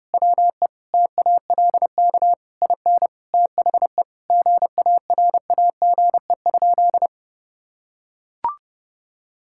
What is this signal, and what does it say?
Morse code, heard as a keyed tone, about 28 wpm, 700 Hz. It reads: WE TALK IN THE GARAGE?
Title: Sentences from Top 1600 Words 30wpm